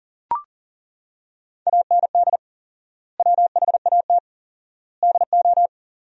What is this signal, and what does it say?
Morse code, heard as a keyed tone, about 37 wpm, 700 Hz. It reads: AND WHAT DO
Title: 4x Word-Spacing - Sentences from Top 100 Words - 40wpm